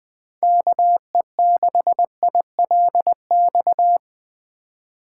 Morse code, heard as a keyed tone, about 20 wpm, 700 Hz. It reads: KE6ILX
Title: USA Call Signs - 6 Characters - 4x Character Spacing - 20wpm